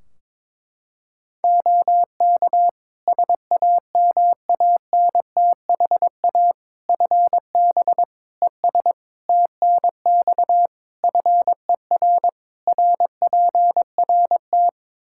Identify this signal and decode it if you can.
Morse code, heard as a keyed tone, about 22 wpm, 700 Hz. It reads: OK SAMANTHA FB ES TNX FER RPRT
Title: Practice QSO - Exchange 2 - Info - 22wpm